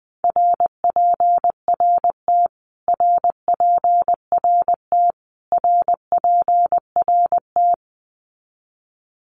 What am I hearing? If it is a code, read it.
Morse code, heard as a keyed tone, about 20 wpm, 700 Hz. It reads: RPRT RPRT RPRT
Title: Abbreviations Repeated 3x - 20wpm